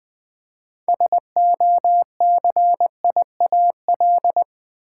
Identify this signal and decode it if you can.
Morse code, heard as a keyed tone, about 21 wpm, 700 Hz. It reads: SOCIAL
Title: Practice QSO - Exchange 3 - More Info - 20wpm